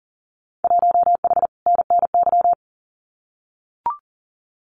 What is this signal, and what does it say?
Morse code, heard as a keyed tone, about 40 wpm, 700 Hz. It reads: 1H NNY
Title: Winter Field Day Exchanges 40wpm